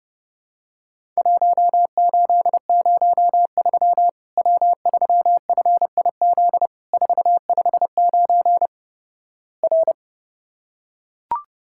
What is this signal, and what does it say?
Morse code, heard as a keyed tone, about 31 wpm, 700 Hz. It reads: 1803 W3FIZ 459 R
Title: Worked All Europe DX Contest - QTC Exchange - 30wpm